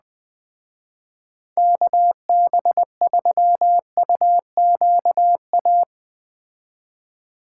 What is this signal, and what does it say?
Morse code, heard as a keyed tone, about 20 wpm, 700 Hz. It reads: KB3UQA